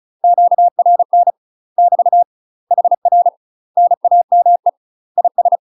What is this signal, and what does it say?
Morse code, heard as a keyed tone, about 34 wpm, 700 Hz. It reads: QRN <BT> HR NAME IS
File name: UK MASTER QSOs TRAINING / 35WPM / UK MASTER QSOs 0001